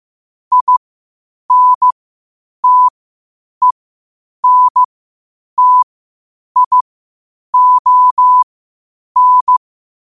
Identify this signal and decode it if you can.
Morse code, heard as a keyed tone, about 15 wpm, spaced as for 8 wpm, 1000 Hz. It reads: INTENTION
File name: qotdfiles / 8wpm